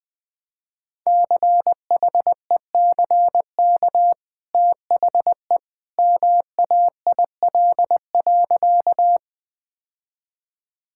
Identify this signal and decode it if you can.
Morse code, heard as a keyed tone, about 20 wpm, 700 Hz. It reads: CHECK THE MAIL.
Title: Easy Sentences with Hints - 20wpm